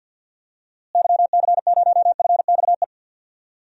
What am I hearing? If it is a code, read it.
Morse code, heard as a keyed tone, about 50 wpm, 700 Hz. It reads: YC0FXE
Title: DXCC-100 Call Signs - 6 Characters - Repeated 2x - 50wpm